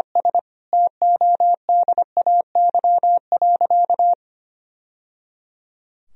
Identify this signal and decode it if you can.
Morse code, heard as a keyed tone, about 25 wpm, 700 Hz. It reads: S TODAY.
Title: Sentences from Top 1000 Words - Review Part 4 - 25wpm